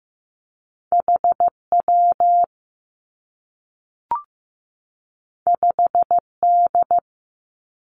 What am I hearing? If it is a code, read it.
Morse code, heard as a keyed tone, about 15 wpm, 700 Hz. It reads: H W 5 D
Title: Two Challenging Letter-Number Combinations 15wpm